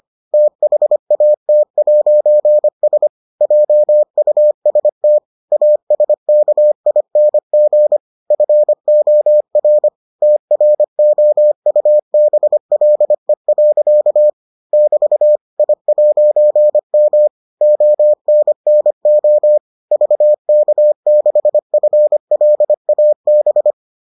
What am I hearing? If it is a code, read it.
Morse code, heard as a keyed tone, about 25 wpm, 600 Hz. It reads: THAT'S JUST ASKING FOR TROUBLE. <BT> I'M ONNO VK6FLAB